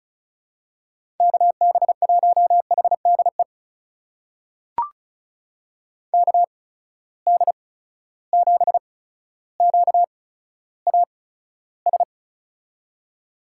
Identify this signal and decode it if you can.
Morse code, heard as a keyed tone, about 36 wpm, spaced as for 15 wpm, 700 Hz. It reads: KB1HDE KD7QAS
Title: USA Call Signs - 6 Characters - 8x Character Spacing - 35wpm